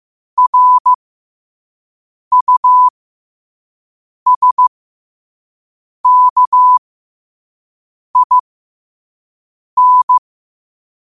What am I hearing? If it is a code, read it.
Morse code, heard as a keyed tone, about 15 wpm, spaced as for 5 wpm, 1000 Hz. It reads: RUSKIN
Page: New quotes every day in morse code at 5 Words per minute.